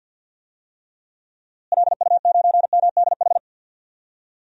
Morse code, heard as a keyed tone, about 46 wpm, 700 Hz. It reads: LU9MDH
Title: DXCC-100 Call Signs - 6 Characters - Repeated 2x - 50wpm